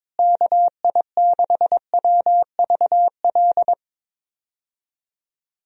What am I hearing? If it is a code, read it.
Morse code, heard as a keyed tone, about 22 wpm, 700 Hz. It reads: KI6WVL
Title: USA Call Signs - 6 Characters - Repeated 3x - 22wpm